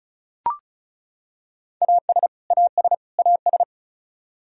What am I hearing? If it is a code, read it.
Morse code, heard as a keyed tone, about 34 wpm, 700 Hz. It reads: AS AS AS
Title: Top 100 Words Repeated 35wpm